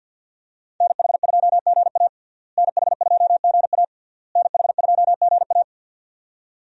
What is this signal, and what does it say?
Morse code, heard as a keyed tone, about 52 wpm, 700 Hz. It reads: NH2GA NH2GA NH2GA
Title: USA Call Signs - 5 Characters - Repeated 3x - 50wpm